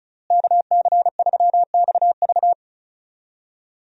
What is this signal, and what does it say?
Morse code, heard as a keyed tone, about 36 wpm, 700 Hz. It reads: KC3XV
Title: USA Call Signs - 5 Characters - Repeated 2x - 35wpm